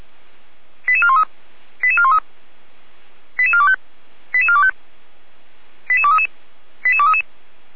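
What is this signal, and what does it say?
Five-tone selective call (ZVEI), heard as quick runs of five tones, tones of about 70 ms. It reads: 88413 88413 88425 88425 88233 88233
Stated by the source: ZVEI five-tone selective call received on SDR.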